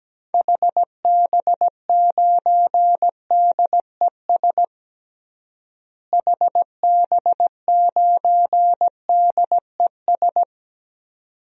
Morse code, heard as a keyed tone, about 17 wpm, 700 Hz. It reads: HB9DES HB9DES
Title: DXCC-100 Call Signs - 6 Characters - Repeated 2x - 17wpm